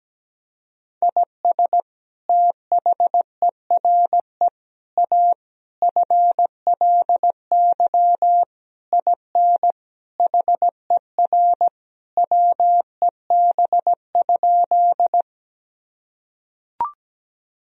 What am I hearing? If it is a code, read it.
Morse code, heard as a keyed tone, about 17 wpm, 700 Hz. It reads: IS THERE A FLY IN HER WEB?
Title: Sentences from Top 1000 Words - Review Part 4 - 17wpm